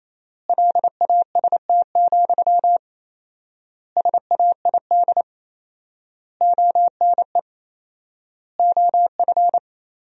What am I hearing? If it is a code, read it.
Morse code, heard as a keyed tone, about 28 wpm, 700 Hz. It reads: LAST, SAID ONE OF
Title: 4x Word-Spacing - Sentences from Top 300 Words - 28wpm